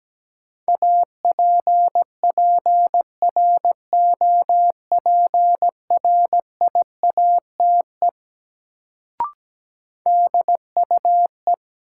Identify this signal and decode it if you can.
Morse code, heard as a keyed tone, about 17 wpm, 700 Hz. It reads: APPROPRIATE DUE
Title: Sentences from Top 900 Words - New Word Drill - 17wpm